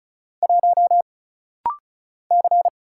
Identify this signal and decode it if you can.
Morse code, heard as a keyed tone, about 36 wpm, 700 Hz. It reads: 1 C
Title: Single Letter-Number - Rapid-Fire 35wpm